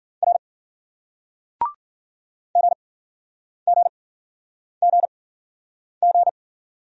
Morse code, heard as a keyed tone, about 52 wpm, spaced as for 9 wpm, 700 Hz. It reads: R GGGG